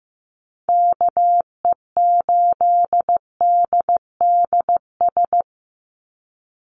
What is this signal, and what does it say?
Morse code, heard as a keyed tone, about 15 wpm, 700 Hz. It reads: KE8DDS
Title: USA Call Signs 15wpm